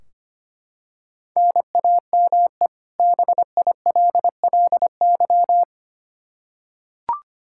Morse code, heard as a keyed tone, about 25 wpm, 700 Hz. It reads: NAME BILLY
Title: Practice QSO - Exchange 1 - Beginning - 25wpm